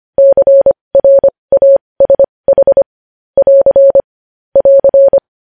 Morse code, heard as a keyed tone, about 25 wpm, spaced as for 20 wpm, 550 Hz. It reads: CRASH <AR> <AR>
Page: News Headlines in Morse Code at 20 WPM